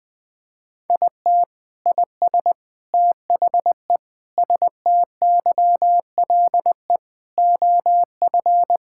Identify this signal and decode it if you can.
Morse code, heard as a keyed tone, about 20 wpm, 700 Hz. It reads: IT IS THE STYLE OF
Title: Sentences from Top 1000 Words - Review Part 5 - 20wpm